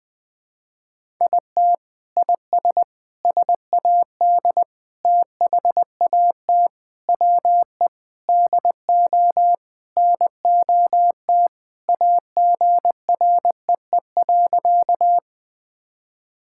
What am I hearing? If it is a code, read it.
Morse code, heard as a keyed tone, about 20 wpm, 700 Hz. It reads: IT IS SAD THAT WE DO NOT AGREE.
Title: Sentences with 4-Word Phrases - 20wpm